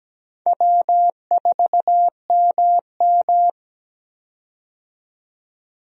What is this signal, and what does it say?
Morse code, heard as a keyed tone, about 17 wpm, 700 Hz. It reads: W4MM
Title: USA Call Signs - 4 Characters - 17wpm